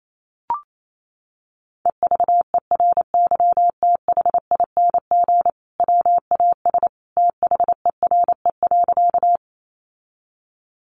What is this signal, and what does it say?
Morse code, heard as a keyed tone, about 28 wpm, 700 Hz. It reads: EVERYTHING WAS THERE.
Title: Sentences from Top 700 Words 28wpm